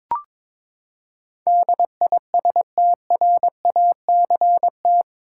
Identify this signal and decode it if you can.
Morse code, heard as a keyed tone, about 21 wpm, 700 Hz. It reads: DISTRACT
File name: practice / 22wpm / Eight Letter Words - 22wpm